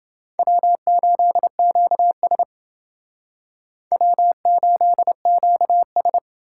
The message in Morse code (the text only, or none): W8QS W8QS